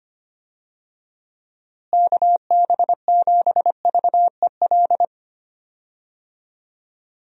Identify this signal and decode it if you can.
Morse code, heard as a keyed tone, about 25 wpm, 700 Hz. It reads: KB7VEL
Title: USA Call Signs - 6 Characters - Repeated 3x - 25wpm